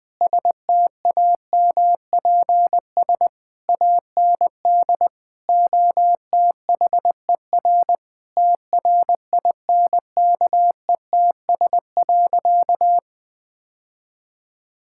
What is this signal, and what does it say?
Morse code, heard as a keyed tone, about 20 wpm, 700 Hz. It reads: STAMPS AND OTHER TRINKETS.
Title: Sentences with 2-Word Phrases - 20wpm